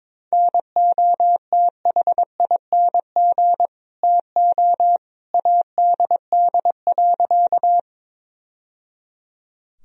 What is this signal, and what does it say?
Morse code, heard as a keyed tone, about 22 wpm, 700 Hz. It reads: NOTHING TO ADD.
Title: Sentences from Top 1000 Words - Review Part 1 - 22wpm